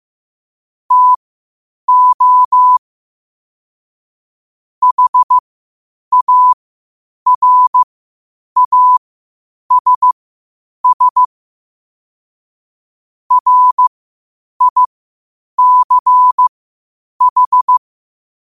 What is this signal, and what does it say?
Morse code, heard as a keyed tone, about 15 wpm, spaced as for 8 wpm, 1000 Hz. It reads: TO HARASS RICH
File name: qotdfiles / 8wpm